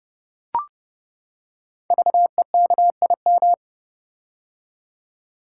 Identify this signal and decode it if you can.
Morse code, heard as a keyed tone, about 28 wpm, 690 Hz. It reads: VEKIM
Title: Code Groups - 5 Letters - 30wpm